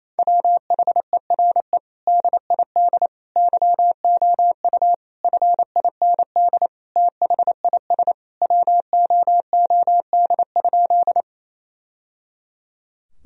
Morse code, heard as a keyed tone, about 27 wpm, 700 Hz. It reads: WHERE DID YOU FIND THIS WOOD?
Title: Sentences from Top 1000 Words - Review Part 2 - 28wpm